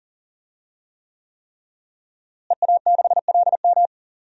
Morse code, heard as a keyed tone, about 40 wpm, 700 Hz. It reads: EA6LM
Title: DXCC-100 Call Signs - 5 Characters - Repeated 3x - 40wpm